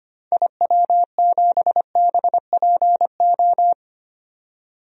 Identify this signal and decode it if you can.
Morse code, heard as a keyed tone, about 25 wpm, 700 Hz. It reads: IW7BPO
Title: DXCC-100 Call Signs - 6 Characters - 25wpm